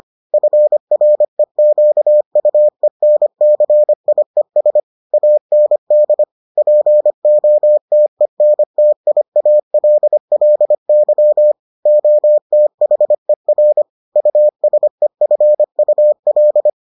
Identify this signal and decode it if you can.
Morse code, heard as a keyed tone, about 25 wpm, 600 Hz. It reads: FREQUENCIES AND POTENTIALLY OTHER USEFUL